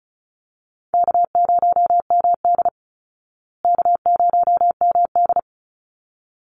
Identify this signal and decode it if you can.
Morse code, heard as a keyed tone, about 34 wpm, 700 Hz. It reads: K0MD K0MD
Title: Easy Call Signs Repeated 2x - 35wpm